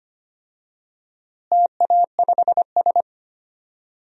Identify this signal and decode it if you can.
Morse code, heard as a keyed tone, about 25 wpm, 700 Hz. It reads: TA5S
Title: DXCC-100 Call Signs - 4 Characters - Repeated 2x - 25wpm